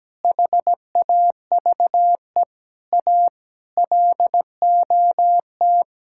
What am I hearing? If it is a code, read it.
Morse code, heard as a keyed tone, about 17 wpm, 700 Hz. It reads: HAVE A LOT
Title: Sentences from Top 1000 Words - Review Part 5 - 17wpm